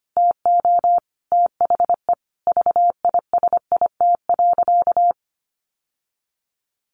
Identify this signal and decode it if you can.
Morse code, heard as a keyed tone, about 25 wpm, 700 Hz. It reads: TO THE VISIT.
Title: Sentences from Top 1200 Words 25wpm